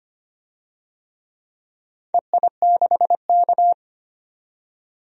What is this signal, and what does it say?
Morse code, heard as a keyed tone, about 25 wpm, 700 Hz. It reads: EI6K
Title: DXCC-100 Call Signs - 4 Characters - 25wpm